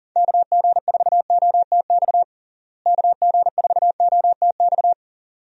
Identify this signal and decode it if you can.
Morse code, heard as a keyed tone, about 40 wpm, 700 Hz. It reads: KG4OTX KG4OTX
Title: USA Call Signs - 6 Characters - Repeated 3x - 40wpm